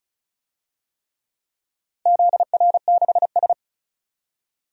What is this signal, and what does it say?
Morse code, heard as a keyed tone, about 34 wpm, 700 Hz. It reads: ZR6S
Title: DXCC-100 Call Signs - 4 Characters - Repeated 3x - 35wpm